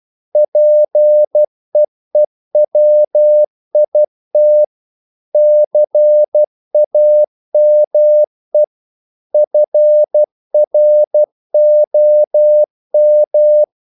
Morse code, heard as a keyed tone, about 12 wpm, 600 Hz. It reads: PEEWIT CAME FROM
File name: mp3 files / the gardener - 12wpm